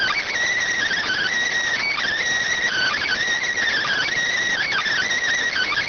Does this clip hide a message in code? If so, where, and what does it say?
Yes: Emergency Alert System (SAME) tones (header bursts) from 0.8 to 5.1 s, heard as harsh buzzing bursts.